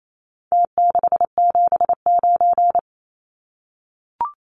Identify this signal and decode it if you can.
Morse code, heard as a keyed tone, about 28 wpm, 700 Hz. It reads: T679
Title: Numbers - Contest Style - 4-Digit 28wpm